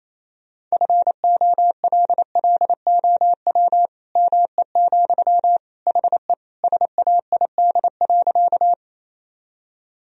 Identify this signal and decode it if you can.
Morse code, heard as a keyed tone, about 28 wpm, 700 Hz. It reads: FOLLOW ME, HE SAID.
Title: Sentences from Top 1000 Words - Review Part 1 - 28wpm